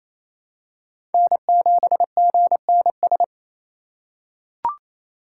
Morse code, heard as a keyed tone, about 28 wpm, 700 Hz. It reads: N7GNS
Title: USA Call Signs - 5 Characters - 2x Character Spacing - 28wpm